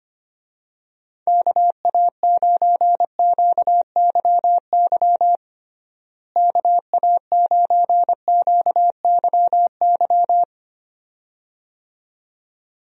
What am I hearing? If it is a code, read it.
Morse code, heard as a keyed tone, about 25 wpm, 700 Hz. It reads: KA9QYY KA9QYY
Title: USA Call Signs - 6 Characters - Repeated 2x - 25wpm